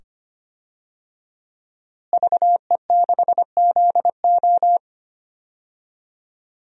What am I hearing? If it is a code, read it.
Morse code, heard as a keyed tone, about 25 wpm, 700 Hz. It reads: VE6ZO
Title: DXCC-100 Call Signs - 5 Characters - Repeated 2x - 25wpm